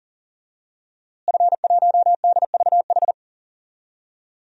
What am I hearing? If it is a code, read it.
Morse code, heard as a keyed tone, about 40 wpm, 700 Hz. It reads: F1DVH
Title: DXCC-100 Call Signs - 5 Characters - Repeated 2x - 40wpm